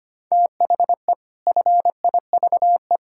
Morse code, heard as a keyed tone, about 25 wpm, 700 Hz. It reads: THE FIVE
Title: Pangram Challenge v2 - 25wpm